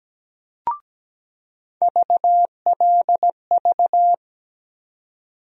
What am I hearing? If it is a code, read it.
Morse code, heard as a keyed tone, about 17 wpm, 700 Hz. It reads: VLV
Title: USA Call Sign Postfixes - 3 Letters 17wpm